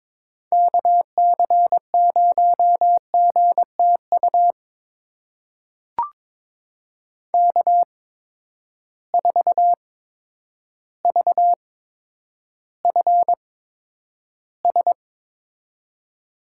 Morse code, heard as a keyed tone, about 22 wpm, spaced as for 10 wpm, 700 Hz. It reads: KC0GTU K4VFS
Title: USA Call Signs - 8x Character Spacing - 22wpm